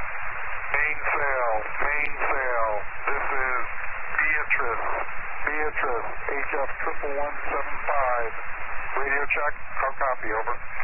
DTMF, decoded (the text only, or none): none